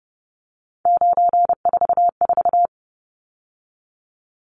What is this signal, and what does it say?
Morse code, heard as a keyed tone, about 30 wpm, 700 Hz. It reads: 944
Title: Numbers - 3-Digit 30wpm